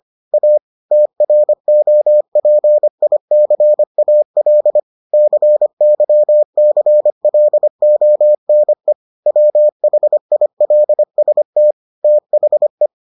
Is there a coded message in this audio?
Yes: Morse code from 0.3 to 12.9 s, heard as a keyed tone, about 25 wpm, 600 Hz. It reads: A TROPICAL CYCLONE WHILST THE